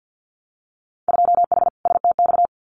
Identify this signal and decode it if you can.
Morse code, heard as a keyed tone, about 50 wpm, 700 Hz. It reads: 2H STX
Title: Winter Field Day Exchanges Encore 50wpm